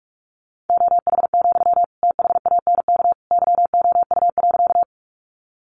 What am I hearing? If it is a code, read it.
Morse code, heard as a keyed tone, about 44 wpm, 700 Hz. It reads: OH, THANK YOU.
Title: Sentences from Top 600 Words 45wpm